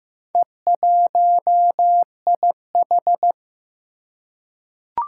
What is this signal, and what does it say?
Morse code, heard as a keyed tone, about 15 wpm, 700 Hz. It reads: E1IH
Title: Four Challenging Letter-Number Combinations 15wpm